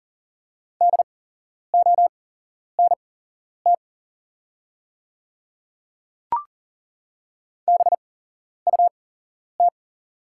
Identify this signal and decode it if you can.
Morse code, heard as a keyed tone, about 40 wpm, spaced as for 11 wpm, 690 Hz. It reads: DONT BUT
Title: 8x Character Spacing - Top 300 Words - 40wpm